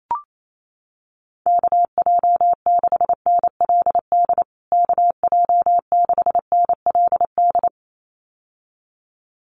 Morse code, heard as a keyed tone, about 28 wpm, 700 Hz. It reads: KJ6NLD KJ6NLD
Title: USA Call Signs Repeated 2x 28wpm